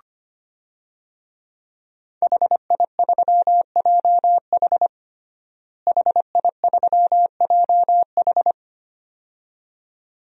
Morse code, heard as a keyed tone, about 25 wpm, 700 Hz. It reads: HI3JH HI3JH
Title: DXCC-100 Call Signs - 5 Characters - Repeated 2x - 25wpm